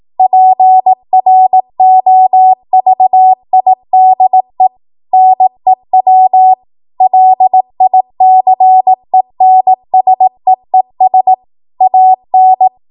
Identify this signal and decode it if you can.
Morse code, heard as a keyed tone, about 18 wpm, 750 Hz. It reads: PROVIDE NEW LICENSEES AN
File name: Morse / 18WPM